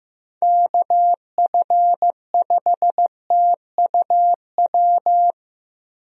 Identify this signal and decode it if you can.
Morse code, heard as a keyed tone, about 16 wpm, 700 Hz. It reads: KF5TUW
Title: USA Call Signs - 6 Characters - Repeated 3x - 15wpm